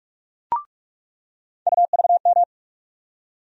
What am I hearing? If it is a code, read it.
Morse code, heard as a keyed tone, about 46 wpm, 700 Hz. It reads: UVM
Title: USA Call Sign Postfixes - 3 Letters 45wpm